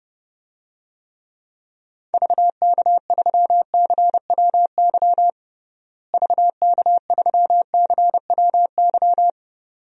Morse code, heard as a keyed tone, about 32 wpm, 700 Hz. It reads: VK3CWY VK3CWY
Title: DXCC-100 Call Signs - 6 Characters - Repeated 2x - 30wpm